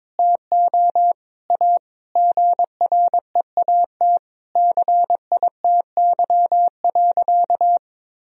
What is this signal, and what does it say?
Morse code, heard as a keyed tone, about 22 wpm, 700 Hz. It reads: TO A GREAT CITY.
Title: Sentences from Top 1000 Words - Review Part 1 - 22wpm